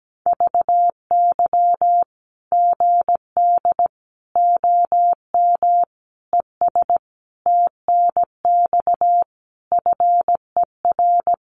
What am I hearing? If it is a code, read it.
Morse code, heard as a keyed tone, about 17 wpm, 700 Hz. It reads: VY GD OM ES TNX FER
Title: Common QSO Phrases 17wpm